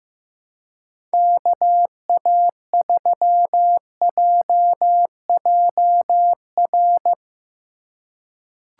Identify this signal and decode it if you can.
Morse code, heard as a keyed tone, about 16 wpm, 700 Hz. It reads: KA3JJR
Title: USA Call Signs - 6 Characters - 2x Character Spacing - 15wpm